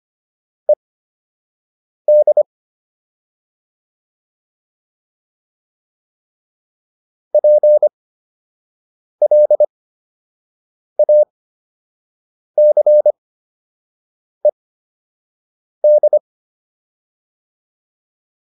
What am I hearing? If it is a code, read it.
Morse code, heard as a keyed tone, about 25 wpm, spaced as for 6 wpm, 600 Hz. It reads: ED PLACED